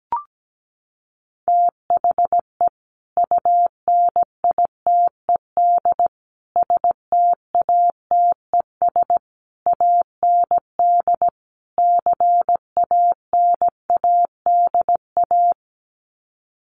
Morse code, heard as a keyed tone, about 17 wpm, 700 Hz. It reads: THE UNITED STATES AND CANADA
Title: Sets of 5 Words 17wpm